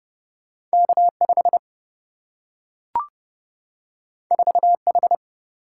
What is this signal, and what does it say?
Morse code, heard as a keyed tone, about 30 wpm, 700 Hz. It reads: K 5 4 H
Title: Single-Letter Single-Number 30wpm